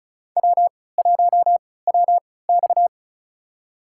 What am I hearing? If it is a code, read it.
Morse code, heard as a keyed tone, about 36 wpm, spaced as for 20 wpm, 700 Hz. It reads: W1WX
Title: USA Call Signs - 4 Characters - 3x Character Spacing - 35wpm